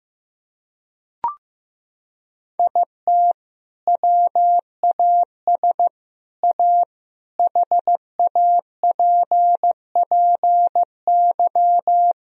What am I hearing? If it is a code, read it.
Morse code, heard as a keyed tone, about 16 wpm, 700 Hz. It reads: IT WAS A HAPPY
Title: Sentences from Top 1000 Words - Review Part 1 - 15wpm